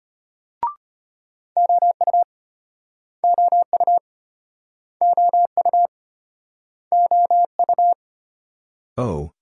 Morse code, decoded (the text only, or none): OU OU OU OU